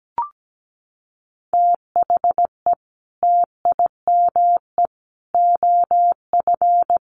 Morse code, heard as a keyed tone, about 17 wpm, 700 Hz. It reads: THE TIME OF
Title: Sets of 3 Words 17wpm